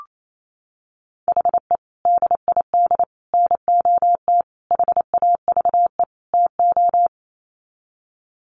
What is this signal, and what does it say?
Morse code, heard as a keyed tone, about 28 wpm, 700 Hz. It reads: HE DID NOT HAVE TO
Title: Sets of 5 Words 28wpm